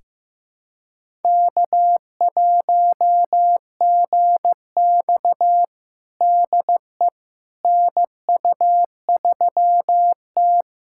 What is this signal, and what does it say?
Morse code, heard as a keyed tone, about 15 wpm, 700 Hz. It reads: K1GX DE NU3T
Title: USA Callsign DE Callsign - 4 Characters - 15wpm